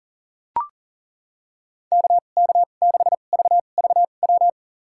Morse code, heard as a keyed tone, about 40 wpm, spaced as for 29 wpm, 700 Hz. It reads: KK6VVW
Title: USA Call Signs - 2x Character Spacing - 40wpm